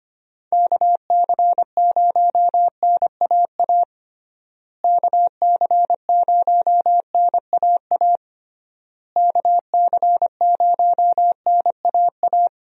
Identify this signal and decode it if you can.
Morse code, heard as a keyed tone, about 25 wpm, 700 Hz. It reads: KC0NAA KC0NAA KC0NAA
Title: USA Call Signs - 6 Characters - Repeated 3x - 25wpm